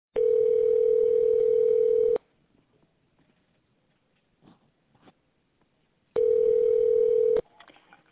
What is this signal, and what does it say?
Ringback tone (North America), 440 + 480 Hz.